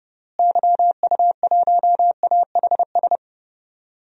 Morse code, heard as a keyed tone, about 32 wpm, 700 Hz. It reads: YU1AHS
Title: DXCC-100 Call Signs - 6 Characters - Repeated 3x - 30wpm